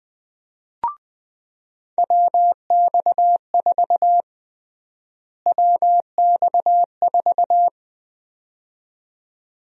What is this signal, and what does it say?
Morse code, heard as a keyed tone, about 20 wpm, 700 Hz. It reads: WX4 WX4
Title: USA Call Sign Prefixes - Repeated 2x - 20wpm